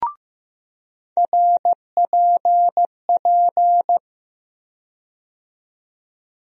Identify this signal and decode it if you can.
Morse code, heard as a keyed tone, about 15 wpm, 700 Hz. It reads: RPP
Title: USA Call Sign Postfixes - 3 Letters 15wpm